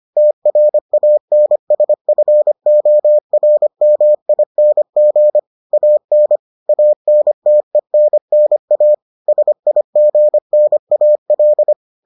Morse code, heard as a keyed tone, about 25 wpm, 600 Hz. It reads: TRANSFORMING AN ANTENNA SIGNAL